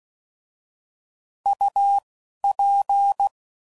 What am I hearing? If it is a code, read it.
Morse code, heard as a keyed tone, about 16 wpm, spaced as for 12 wpm, 800 Hz. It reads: UP
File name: Prefixes / By First Letter / Us